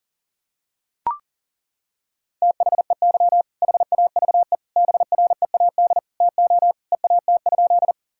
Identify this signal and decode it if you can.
Morse code, heard as a keyed tone, about 40 wpm, 700 Hz. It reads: THEY HAVE BREAD TO EAT?
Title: Sentences from Top 1600 Words 40wpm